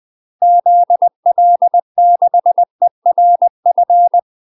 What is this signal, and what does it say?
Morse code, heard as a keyed tone, about 20 wpm, 700 Hz. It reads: ZL6ERF
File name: UK MASTER QSOs TRAINING/20WPM